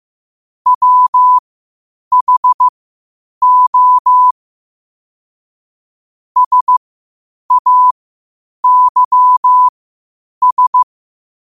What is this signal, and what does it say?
Morse code, heard as a keyed tone, about 15 wpm, spaced as for 8 wpm, 1000 Hz. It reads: WHO SAYS
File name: qotdfiles / 8wpm